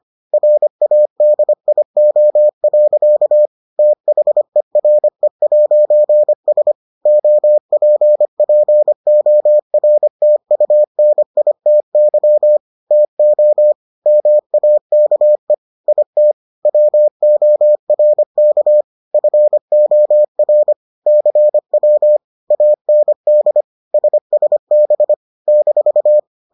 Morse code, heard as a keyed tone, about 25 wpm, 600 Hz. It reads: RADIO. THERE'S OPPORTUNITY TO MAKE IT WORK FOR CW AND SSB -